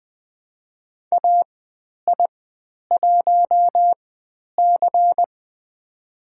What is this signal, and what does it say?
Morse code, heard as a keyed tone, about 20 wpm, spaced as for 10 wpm, 700 Hz. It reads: AI1C
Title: 095 Lesson 16g - Letter c - Call Signs 20wpm